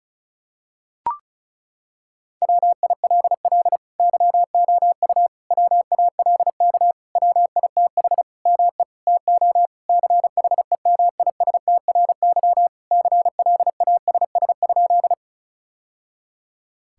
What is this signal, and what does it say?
Morse code, heard as a keyed tone, about 36 wpm, 700 Hz. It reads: WILL YOU WALK WITH ME TO CHEMISTRY CLASS?
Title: Sentences from Top 1000 Words - Review Part 6 - 35wpm